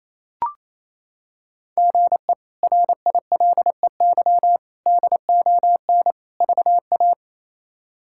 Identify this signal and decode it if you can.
Morse code, heard as a keyed tone, about 28 wpm, 700 Hz. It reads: GE RILEY DON VA
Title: K1USN SST QSO Practice - 28wpm